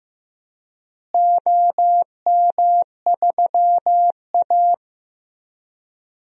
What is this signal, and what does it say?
Morse code, heard as a keyed tone, about 15 wpm, 700 Hz. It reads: OM3A
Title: DXCC-100 Call Signs - 4 Characters - Repeated 3x - 15wpm